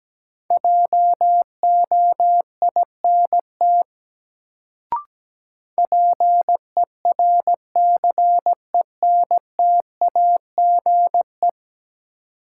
Morse code, heard as a keyed tone, about 17 wpm, 700 Hz. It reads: JOINT PERCENTAGE
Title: Sentences from Top 1300 Words - New Word Drill - 17wpm